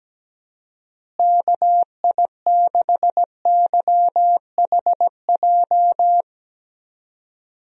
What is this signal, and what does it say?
Morse code, heard as a keyed tone, about 17 wpm, 700 Hz. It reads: KI6YHJ
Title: USA Call Signs - 6 Characters - Repeated 2x - 17wpm